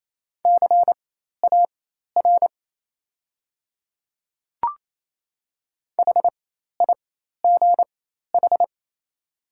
Morse code, heard as a keyed tone, about 28 wpm, spaced as for 13 wpm, 700 Hz. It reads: CAR HIGH
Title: 4x Character Spacing - Top 300 Words - 28wpm